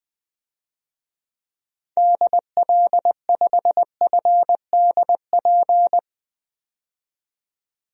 Morse code, heard as a keyed tone, about 21 wpm, 700 Hz. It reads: DL5FDP
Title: DXCC-100 Call Signs - 6 Characters - 20wpm